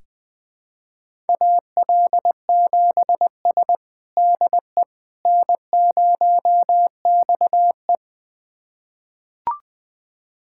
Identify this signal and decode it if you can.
Morse code, heard as a keyed tone, about 21 wpm, 700 Hz. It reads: AL7S DE N0XE
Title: USA Callsign DE Callsign - 4 Characters - 20wpm